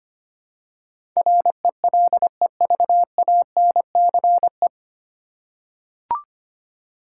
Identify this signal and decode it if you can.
Morse code, heard as a keyed tone, about 25 wpm, 700 Hz. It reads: RELEVANCE
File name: practice / 25wpm / 9 Letter Words - First 3 Letters Given - 25wpm